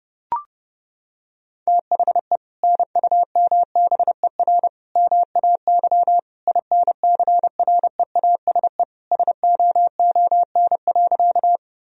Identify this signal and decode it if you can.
Morse code, heard as a keyed tone, about 32 wpm, 700 Hz. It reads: THE NUMBER MAY INCREASE SOON.
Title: Sentences from Top 1000 Words - Review Part 4 - 30wpm